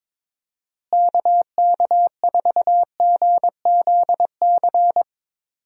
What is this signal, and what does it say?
Morse code, heard as a keyed tone, about 22 wpm, 700 Hz. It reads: KK4GZC
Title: USA Call Signs - 6 Characters - Repeated 2x - 22wpm